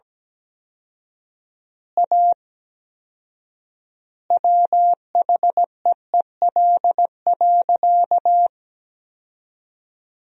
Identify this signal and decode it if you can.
Morse code, heard as a keyed tone, about 17 wpm, 700 Hz. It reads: A WHEEL.
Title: 4x Word-Spacing - Sentences from Top 500 Words - 17wpm